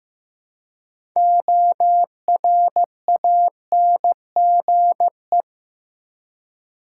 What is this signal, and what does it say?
Morse code, heard as a keyed tone, about 15 wpm, 700 Hz. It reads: ORANGE
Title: Six Letter Words - 15wpm